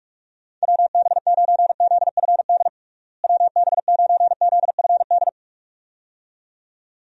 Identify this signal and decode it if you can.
Morse code, heard as a keyed tone, about 48 wpm, 700 Hz. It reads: WB9ZFD WB9ZFD
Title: USA Call Signs - 6 Characters - Repeated 2x - 45wpm